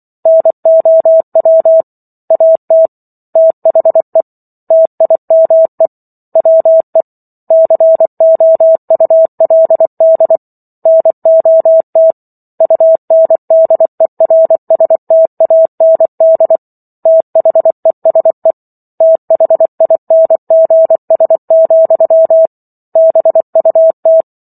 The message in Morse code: NOW AT THE TIME WE COULD NOT UNDERSTAND THESE THINGS, BUT

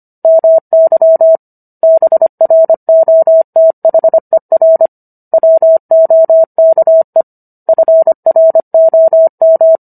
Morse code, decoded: MY BROTHER WOKE FROM

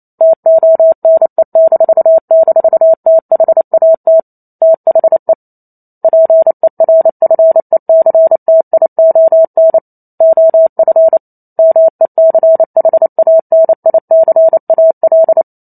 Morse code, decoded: TONE--THAT THE PERFECTION OF MECHANICAL

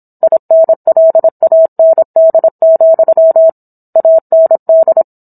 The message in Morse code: INLAND, AND

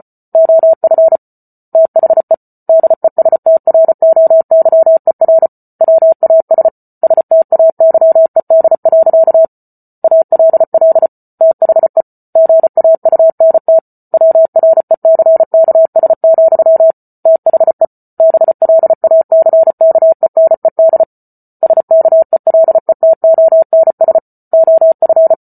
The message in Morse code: OF THE DESTROYER WAS STAYED. ALL THE GAUNT WRECKS, THE BLACKENED SKELETONS OF

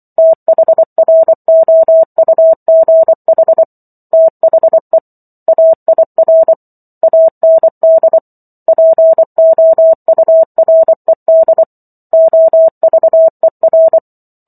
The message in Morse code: THROUGH THE AIR AND POURED OVER

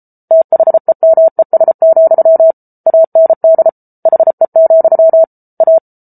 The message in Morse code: THEMES, AND HE, A

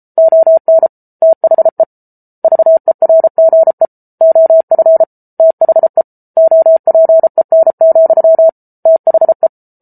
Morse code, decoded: ON THE VERGE OF THE OPEN, THE

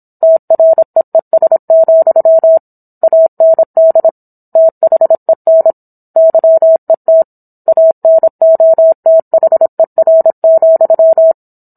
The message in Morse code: TREES, AND THEN YET ANOTHER,

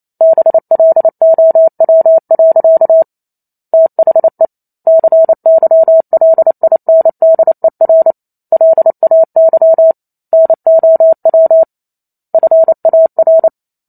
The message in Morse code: BLOW. THE CYLINDER LAY NOW FAR

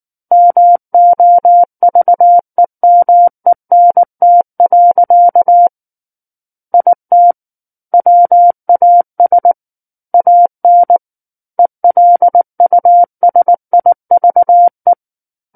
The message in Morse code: MOVEMENT. IT WAS AN ELUSIVE

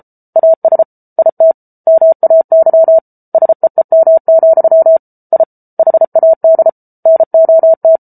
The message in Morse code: AS IT MAY SEEM, I HAD NOT

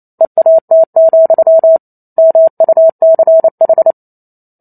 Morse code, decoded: EAT, MUCH

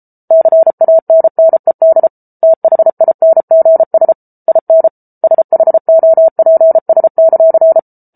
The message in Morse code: CANNED THINGS IN SHOPS;

